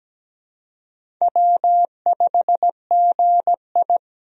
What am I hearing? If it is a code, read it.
Morse code, heard as a keyed tone, about 17 wpm, 700 Hz. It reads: W5GI
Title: USA Call Signs - 4 Characters - 17wpm